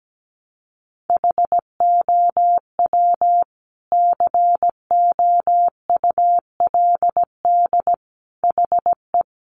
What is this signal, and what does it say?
Morse code, heard as a keyed tone, about 17 wpm, 700 Hz. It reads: HOW COULD HE
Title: Sentences from Top 1300 Words 17wpm